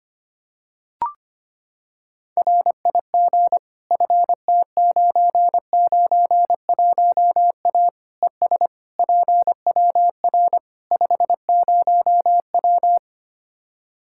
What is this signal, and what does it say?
Morse code, heard as a keyed tone, about 25 wpm, 700 Hz. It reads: RIG FT991A ES PWR 50W
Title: Practice QSO - Exchange 2 - Info - 25wpm